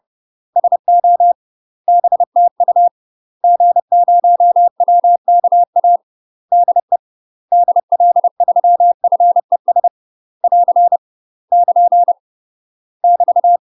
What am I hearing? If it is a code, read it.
Morse code, heard as a keyed tone, about 30 wpm, 700 Hz. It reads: SO BTU G0WKA DE DL3FES <AR> <KN> <BT>
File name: UK MASTER QSOs TRAINING / 30WPM / UK MASTER QSOs 0018